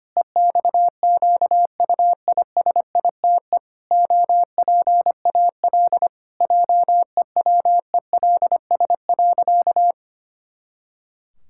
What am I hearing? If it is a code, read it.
Morse code, heard as a keyed tone, about 25 wpm, 700 Hz. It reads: EXQUISITE OPAL JEWELS.
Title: Pangram Challenge v2 - 25wpm